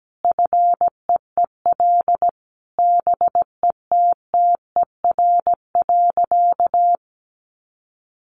Morse code, heard as a keyed tone, about 17 wpm, 700 Hz. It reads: FEEL BETTER.
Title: Easy Phrases 17wpm